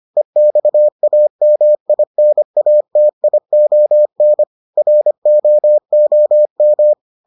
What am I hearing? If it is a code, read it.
Morse code, heard as a keyed tone, about 25 wpm, 600 Hz. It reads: EXAMINATION ROOM